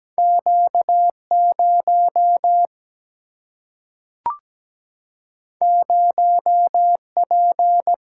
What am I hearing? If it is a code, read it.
Morse code, heard as a keyed tone, about 17 wpm, 700 Hz. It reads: Q0 0P
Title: Single-Letter Single-Number 17wpm